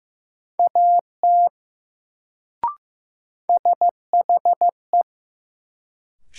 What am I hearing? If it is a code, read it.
Morse code, heard as a keyed tone, about 15 wpm, 700 Hz. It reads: AT SHE
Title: Sentences from Top 100 Words - New Word Drill - 15wpm